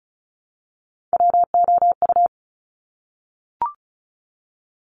Morse code, heard as a keyed tone, about 36 wpm, 700 Hz. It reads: WOU
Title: Top Three Letter Combinations in Words 35wpm